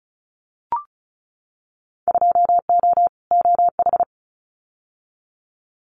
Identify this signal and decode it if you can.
Morse code, heard as a keyed tone, about 36 wpm, 700 Hz. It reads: 2O OH
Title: Winter Field Day Exchanges 35wpm